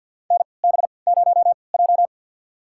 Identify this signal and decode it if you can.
Morse code, heard as a keyed tone, about 52 wpm, spaced as for 29 wpm, 700 Hz. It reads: NB0J
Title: USA Call Signs - 4 Characters - 3x Character Spacing - 50wpm